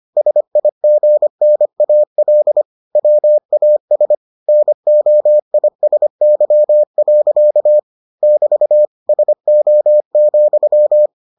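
Morse code, heard as a keyed tone, about 25 wpm, 600 Hz. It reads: SIGNAL WAS NOISY. <BT> SO,
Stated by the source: Morse edition of the Foundations of Amateur Radio episodes.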